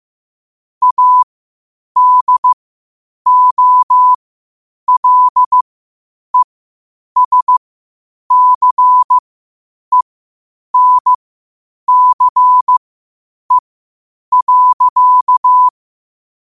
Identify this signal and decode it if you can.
Morse code, heard as a keyed tone, about 15 wpm, spaced as for 8 wpm, 1000 Hz. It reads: ADOLESCENCE.
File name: qotdfiles / 8wpm